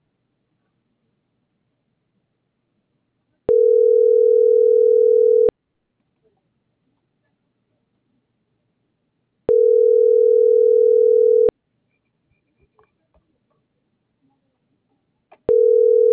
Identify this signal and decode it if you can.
Ringback tone (North America), 440 + 480 Hz.